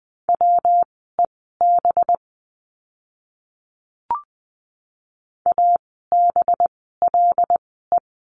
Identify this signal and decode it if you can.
Morse code, heard as a keyed tone, about 20 wpm, spaced as for 14 wpm, 700 Hz. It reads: WEB ABLE
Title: 185 Lesson 30f - Letter b - Words 20wpm